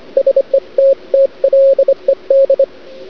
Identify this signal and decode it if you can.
Morse code, heard as a keyed tone, about 25 wpm, 540 Hz. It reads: SETTLED
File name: cd-odj / Suara / cwrag